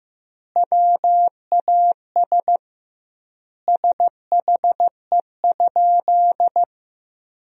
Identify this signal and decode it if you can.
Morse code, heard as a keyed tone, about 15 wpm, 700 Hz. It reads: WAS SHE?
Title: 2x Word-Spacing - Sentences from Top 100 Words - 15wpm